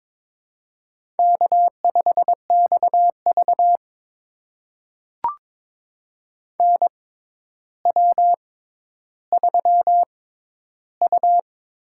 Morse code, heard as a keyed tone, about 22 wpm, spaced as for 11 wpm, 700 Hz. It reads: K5XV NW3U
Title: USA Call Signs - 4 Characters - 6x Character Spacing - 22wpm